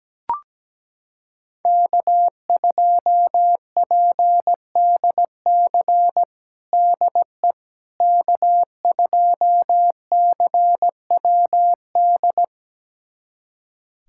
Morse code, heard as a keyed tone, about 17 wpm, 700 Hz. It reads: K2PDC DE K2CWD
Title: USA Callsign DE Callsign - 5 Characters - 17wpm